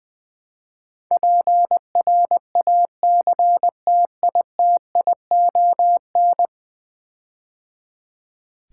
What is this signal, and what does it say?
Morse code, heard as a keyed tone, about 20 wpm, 700 Hz. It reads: PRACTITION
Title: Build A Word - 5 Letters At A Time - 12-Letter Words - 20wpm